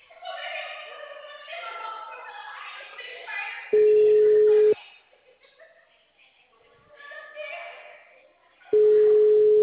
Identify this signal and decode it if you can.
Ringback tone (Europe, 425 Hz).
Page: Rio Claro 3